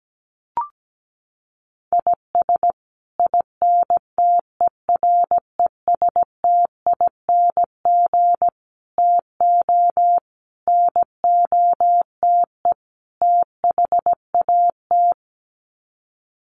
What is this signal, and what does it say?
Morse code, heard as a keyed tone, about 17 wpm, 700 Hz. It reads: IS INTERESTING TO NOTE THAT
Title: Sets of 5 Words Encore2 17wpm